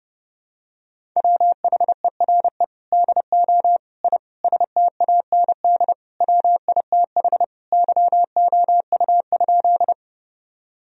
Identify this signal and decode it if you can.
Morse code, heard as a keyed tone, about 30 wpm, 700 Hz. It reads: WHERE DO I STAND WITH YOU?
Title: Sentences from Top 1000 Words - Review Part 1 - 30wpm